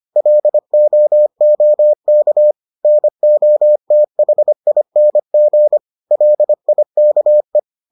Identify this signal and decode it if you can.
Morse code, heard as a keyed tone, about 25 wpm, 600 Hz. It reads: LOOK NOTHING LIKE